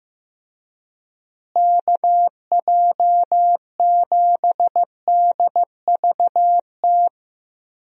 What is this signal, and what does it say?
Morse code, heard as a keyed tone, about 15 wpm, 700 Hz. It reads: KJ7DVT
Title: USA Call Signs - 6 Characters - Repeated 3x - 15wpm